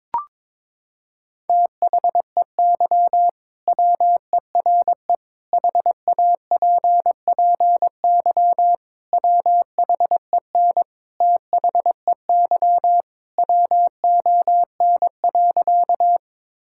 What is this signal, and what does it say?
Morse code, heard as a keyed tone, about 22 wpm, 700 Hz. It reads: THEY WERE HAPPY WHEN THEY WON.
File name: practice / 22wpm / Sentences with 2-Word Phrases - 22wpm